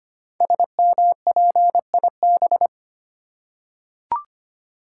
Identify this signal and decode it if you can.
Morse code, heard as a keyed tone, about 25 wpm, 700 Hz. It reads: SMPIB
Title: Code Groups - 5 Letters - 25wpm